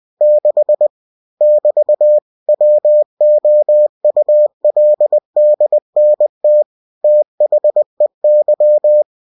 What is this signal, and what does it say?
Morse code, heard as a keyed tone, about 20 wpm, 600 Hz. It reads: 6 <BT> WOULDNT THEY
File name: cw-books / winnie-the-pooh / winnie-20wpm-page-0006